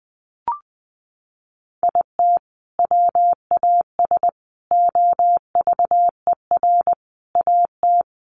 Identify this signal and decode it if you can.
Morse code, heard as a keyed tone, about 20 wpm, 700 Hz. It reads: IT WAS OVER AT
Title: Sentences from Top 600 Words 20wpm